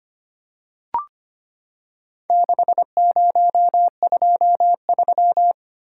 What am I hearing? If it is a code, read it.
Morse code, heard as a keyed tone, about 25 wpm, 700 Hz. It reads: 6023
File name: practice / 25wpm / Four Letter-Number Combinations - 25wpm